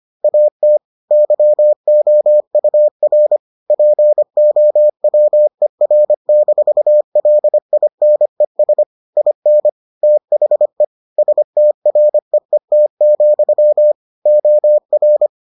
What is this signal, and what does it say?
Morse code, heard as a keyed tone, about 25 wpm, 600 Hz. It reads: AT YOUR POWER-LINES IN THE STREET, OR